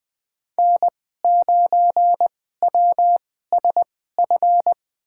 Morse code, heard as a keyed tone, about 20 wpm, spaced as for 14 wpm, 700 Hz. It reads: N9WSF
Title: USA Call Signs - 5 Characters - 2x Character Spacing - 20wpm